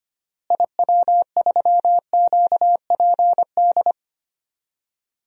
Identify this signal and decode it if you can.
Morse code, heard as a keyed tone, about 25 wpm, 700 Hz. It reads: IW3QPD
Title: DXCC-100 Call Signs - 6 Characters - Repeated 3x - 25wpm